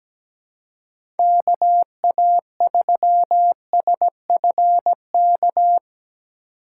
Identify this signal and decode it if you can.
Morse code, heard as a keyed tone, about 17 wpm, 700 Hz. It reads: K A 3 S F K
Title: USA Call Signs - 6 Characters - 6x Character Spacing - 17wpm